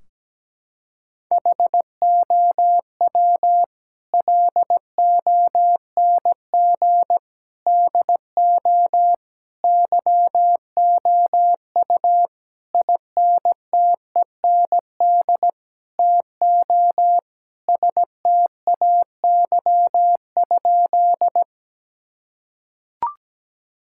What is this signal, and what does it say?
Morse code, heard as a keyed tone, about 17 wpm, 700 Hz. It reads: HOW LONG DO YOU INTEND TO STAY?
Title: Sentences from Top 1600 Words 17wpm